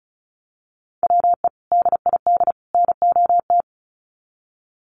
Morse code, heard as a keyed tone, about 34 wpm, 700 Hz. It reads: WE DID NOT
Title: Sets of 3 Words 35wpm